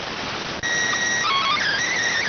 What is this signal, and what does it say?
SSTV (slow-scan television), PD 290.